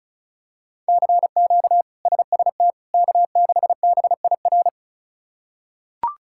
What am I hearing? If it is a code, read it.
Morse code, heard as a keyed tone, about 34 wpm, 700 Hz. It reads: CQ SST K6BIR
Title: K1USN SST QSO Practice - 35wpm